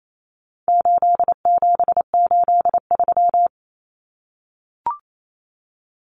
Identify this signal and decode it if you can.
Morse code, heard as a keyed tone, about 28 wpm, 700 Hz. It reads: 8783
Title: Numbers - 4-Digit 28wpm